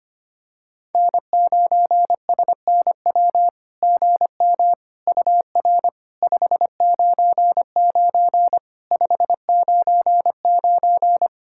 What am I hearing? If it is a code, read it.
Morse code, heard as a keyed tone, about 25 wpm, 700 Hz. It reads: N9SNW GM UR 599 599
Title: SOTA QSO Practice 25wpm